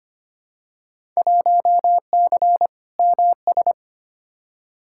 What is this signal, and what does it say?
Morse code, heard as a keyed tone, about 25 wpm, 700 Hz. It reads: 1C MS
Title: Field Day Exchanges Encore - 25wpm